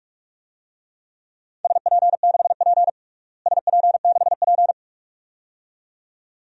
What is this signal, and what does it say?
Morse code, heard as a keyed tone, about 46 wpm, 700 Hz. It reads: SP6P SP6P
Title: DXCC-100 Call Signs - 4 Characters - Repeated 2x - 45wpm